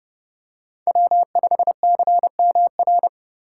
Morse code, heard as a keyed tone, about 27 wpm, 700 Hz. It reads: W5CMR